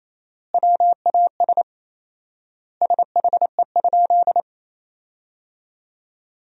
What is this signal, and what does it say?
Morse code, heard as a keyed tone, about 28 wpm, 700 Hz. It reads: WAS SHE?
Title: 4x Word-Spacing - Sentences from Top 100 Words - 28wpm